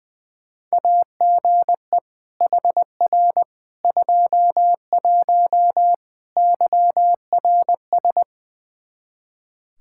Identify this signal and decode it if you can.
Morse code, heard as a keyed tone, about 20 wpm, 700 Hz. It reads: AGE HR 21 YRS
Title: Practice QSO - Exchange 3 - More Info - 20wpm